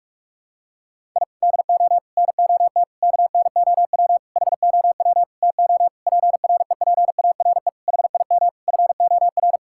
Morse code, heard as a keyed tone, about 41 wpm, 700 Hz. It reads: I DO NOT KNOW HOW TO PREPARE HIM FOR
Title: Sentences from Top 1000 Words - Review Part 6 - 45wpm